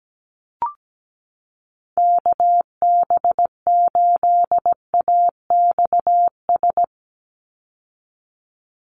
Morse code, heard as a keyed tone, about 17 wpm, 700 Hz. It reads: KB8AXS
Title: USA Call Signs 17wpm